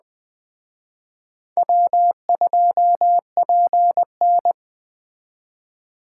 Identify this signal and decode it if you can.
Morse code, heard as a keyed tone, about 20 wpm, 700 Hz. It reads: W2PN
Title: Speed Racing - USA 1x2 Call Signs 17wpm